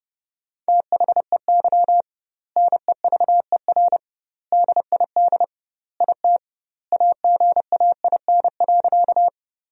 Morse code, heard as a keyed tone, about 30 wpm, 700 Hz. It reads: THEY NEVER DID IT AGAIN.
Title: 2x Word-Spacing - Sentences from Top 300 Words - 30wpm